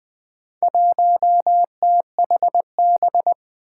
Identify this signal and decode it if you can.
Morse code, heard as a keyed tone, about 20 wpm, 700 Hz. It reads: 1THB
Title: Four Challenging Letter-Number Combinations 20wpm